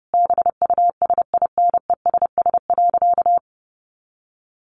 Morse code, heard as a keyed tone, about 30 wpm, 700 Hz. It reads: BUSINESS.
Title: Sentences from Top 1300 Words 30wpm